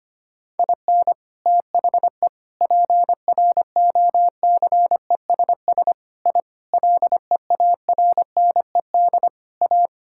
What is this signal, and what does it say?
Morse code, heard as a keyed tone, about 25 wpm, 700 Hz. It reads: IN THE PROCESS I LEARNED A
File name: practice / 25wpm / Sentences with 3-Word Phrases - 25wpm